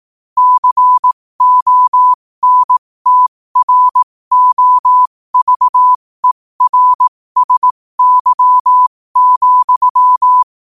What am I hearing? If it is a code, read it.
Morse code, heard as a keyed tone, about 18 wpm, spaced as for 15 wpm, 1000 Hz. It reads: CONTROVERSY,
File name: qotdfiles / 15wpm